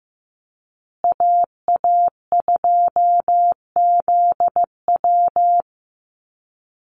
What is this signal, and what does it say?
Morse code, heard as a keyed tone, about 15 wpm, 700 Hz. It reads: AA2ZW
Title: Call Signs Repeated 2x - 15wpm